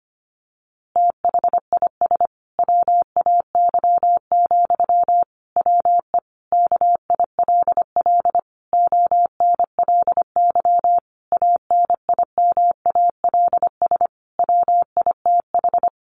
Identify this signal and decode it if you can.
Morse code, heard as a keyed tone, about 25 wpm, 700 Hz. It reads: THIS WAY, WE KILL ONLY ANIMALS WITH
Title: A Sound Of Thunder 25wpm